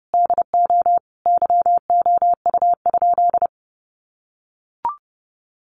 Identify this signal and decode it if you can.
Morse code, heard as a keyed tone, about 30 wpm, 700 Hz. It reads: DO YOU?
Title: Sentences from Top 1200 Words 30wpm